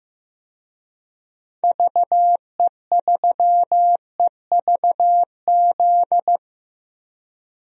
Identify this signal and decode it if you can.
Morse code, heard as a keyed tone, about 15 wpm, 700 Hz. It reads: VE3EVZ
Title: DXCC-100 Call Signs - 6 Characters - Repeated 3x - 15wpm